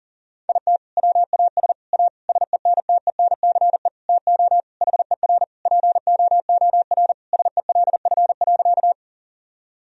Morse code, heard as a keyed tone, about 40 wpm, 700 Hz. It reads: IT WAS A SENTENCE TO HER POOR SELF.
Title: Sentences from Top 1000 Words - Review Part 1 - 40wpm